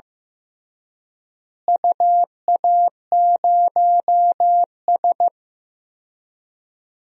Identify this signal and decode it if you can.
Morse code, heard as a keyed tone, about 15 wpm, 700 Hz. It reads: UA0S
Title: DXCC-100 Call Signs - 4 Characters - Repeated 3x - 15wpm